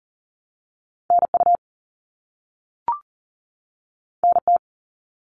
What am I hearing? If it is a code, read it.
Morse code, heard as a keyed tone, about 38 wpm, 700 Hz. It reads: NU NT
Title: Canadian Provinces and Territories 40wpm